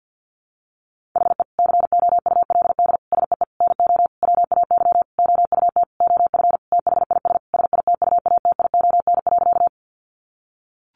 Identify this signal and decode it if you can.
Morse code, heard as a keyed tone, about 50 wpm, 700 Hz. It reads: HE COULD SEE NO WAY OUT OF THIS SITUATION.
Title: Sentences from Top 1200 Words 50wpm